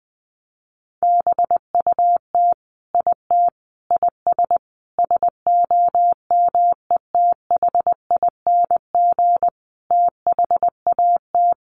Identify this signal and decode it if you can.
Morse code, heard as a keyed tone, about 20 wpm, 700 Hz. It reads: BUT IT IS SOMETHING THAT
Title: Sentences from Top 1000 Words 20wpm